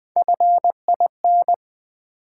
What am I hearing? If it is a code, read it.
Morse code, heard as a keyed tone, about 20 wpm, 700 Hz. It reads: FIN
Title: Build A Word - 1 Letter At A Time - 3 to 5-Letter Words - 20wpm